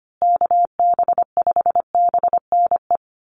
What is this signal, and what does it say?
Morse code, heard as a keyed tone, about 25 wpm, 700 Hz. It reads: KB5BNE
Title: USA Call Signs Repeated 3x 25wpm